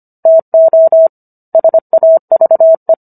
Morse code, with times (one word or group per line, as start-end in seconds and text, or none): 0.3-1.1 TO
1.5-2.9 SAVE